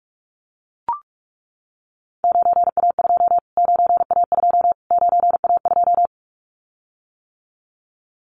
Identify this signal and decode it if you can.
Morse code, heard as a keyed tone, about 44 wpm, 700 Hz. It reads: 9A2 9A2 9A2
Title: Callsign Prefixes for Top 25 DXCC - Repeated 3x - 45wpm